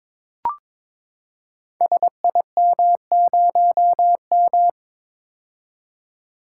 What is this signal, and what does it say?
Morse code, heard as a keyed tone, about 22 wpm, 700 Hz. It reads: SIM0M
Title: Code Groups - 5 Letter:Numbers - 22wpm